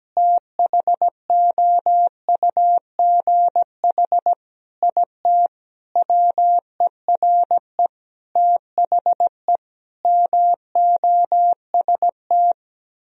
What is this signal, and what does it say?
Morse code, heard as a keyed tone, about 17 wpm, 700 Hz. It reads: THOUGH IT WERE THE MOST
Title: Sentences from Top 1000 Words - Review Part 3 - 17wpm